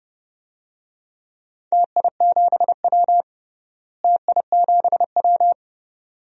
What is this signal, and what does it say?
Morse code, heard as a keyed tone, about 30 wpm, 700 Hz. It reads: TI7W TI7W
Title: DXCC-100 Call Signs - 4 Characters - Repeated 2x - 30wpm